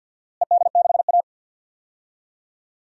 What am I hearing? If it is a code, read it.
Morse code, heard as a keyed tone, about 47 wpm, 700 Hz. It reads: ED6A
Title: DXCC-100 Call Signs - 4 Characters - Repeated 2x - 50wpm